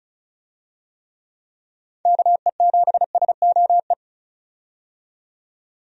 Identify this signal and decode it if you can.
Morse code, heard as a keyed tone, about 36 wpm, 700 Hz. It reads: KE7SOE
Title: USA Call Signs - 6 Characters - 35wpm